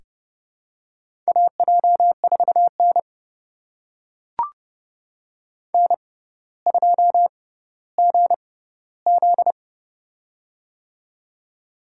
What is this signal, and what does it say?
Morse code, heard as a keyed tone, about 30 wpm, spaced as for 15 wpm, 700 Hz. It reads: AJ4N N2GZ
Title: USA Call Signs - 6x Character Spacing - 30wpm